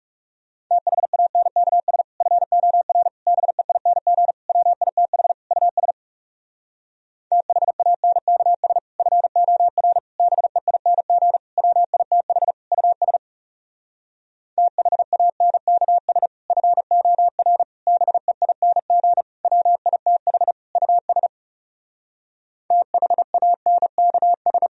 Morse code, heard as a keyed tone, about 37 wpm, 700 Hz. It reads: THANKS FOR BEING WITH US THANKS FOR BEING WITH US THANKS FOR BEING WITH US THANKS
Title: Speed-Racing - Sets of 5 Words 30wpm